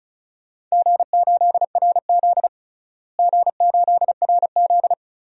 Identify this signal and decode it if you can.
Morse code, heard as a keyed tone, about 36 wpm, 700 Hz. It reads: G8RZ G8RZ
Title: DXCC-100 Call Signs - 4 Characters - Repeated 3x - 35wpm